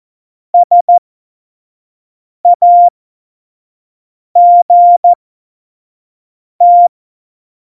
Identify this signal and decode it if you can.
Morse code, heard as a keyed tone, about 14 wpm, spaced as for 5 wpm, 700 Hz. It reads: SAGT
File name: Morsekurs / Witze / Witz 07 April 2020, 25 F70